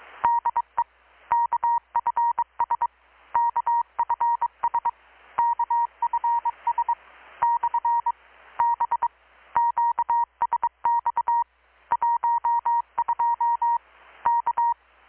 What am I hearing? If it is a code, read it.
Morse code, heard as a keyed tone, about 23 wpm, 940 Hz. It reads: DE KFS KFS KFS / B QSX 12 K
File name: blog / swl / 6-abril-2019-12695-USB-KPS